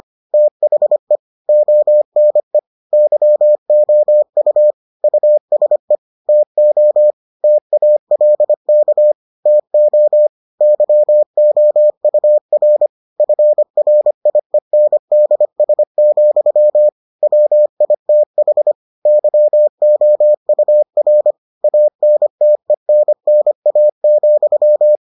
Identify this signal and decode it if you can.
Morse code, heard as a keyed tone, about 25 wpm, 600 Hz. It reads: THE ONE YOU USE TO TALK TO YOUR FRIENDS, WITH YOUR ANTENNA,